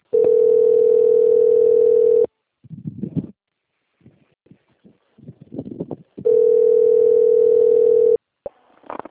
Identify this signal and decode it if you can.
Ringback tone (North America), 440 + 480 Hz.